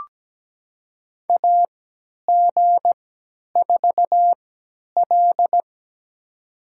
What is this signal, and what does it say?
Morse code, heard as a keyed tone, about 17 wpm, spaced as for 10 wpm, 700 Hz. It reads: AG4L
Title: USA Call Signs - 3x Character Spacing - 17wpm